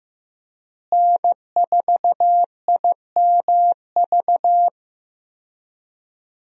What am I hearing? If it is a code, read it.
Morse code, heard as a keyed tone, about 15 wpm, 700 Hz. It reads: N4IMV
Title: USA Call Signs - 5 Characters - Repeated 2x - 15wpm